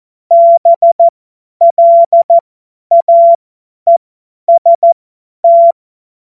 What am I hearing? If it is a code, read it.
Morse code, heard as a keyed tone, about 14 wpm, spaced as for 10 wpm, 680 Hz. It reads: BLAEST
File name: Morsekurs / Witze / Witz 02. Februar 2021, 50 F70